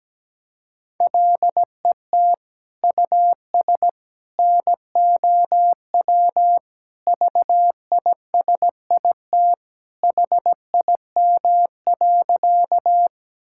Morse code, heard as a keyed tone, about 17 wpm, 700 Hz. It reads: LET US NOW VISIT HIM.
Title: Sentences from Top 1000 Words - Review Part 6 - 17wpm